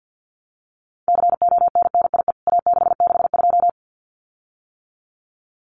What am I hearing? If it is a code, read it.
Morse code, heard as a keyed tone, about 50 wpm, 700 Hz. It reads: CONNIE A662
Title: CWT Exchange 50wpm